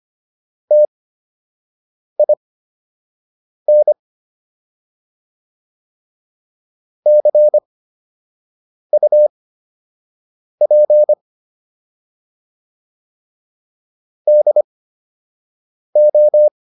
Morse code, heard as a keyed tone, about 25 wpm, spaced as for 6 wpm, 600 Hz. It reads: TIN CUP DO